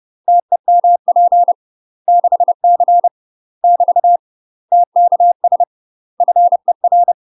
Morse code, heard as a keyed tone, about 30 wpm, 700 Hz. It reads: TEMP 6C <BT> TKS FER
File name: UK MASTER QSOs TRAINING / 30WPM / UK MASTER QSOs 0015